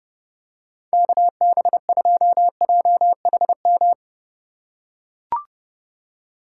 Morse code, heard as a keyed tone, about 30 wpm, 700 Hz. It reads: KB2JHM
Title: USA Call Signs - 6 Characters - 4x Character Spacing - 30wpm